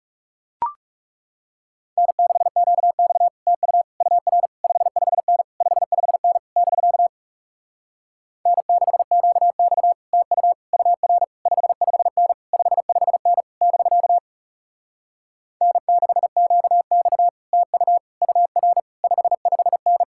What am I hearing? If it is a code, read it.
Morse code, heard as a keyed tone, about 40 wpm, 700 Hz. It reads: N6QX TU UR 55N 55N <BK> N6QX TU UR 55N 55N <BK> N6QX TU UR 55N